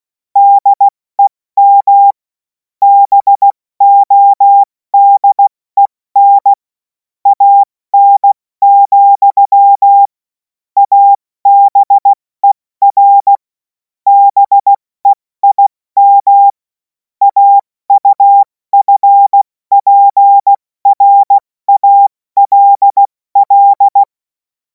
Morse code, heard as a keyed tone, about 16 wpm, spaced as for 14 wpm, 800 Hz. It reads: DEM BODEN AN, ABER BEIM AUFPRALL
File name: Morsekurs / Witze / Witz 11. August 2026 70 F80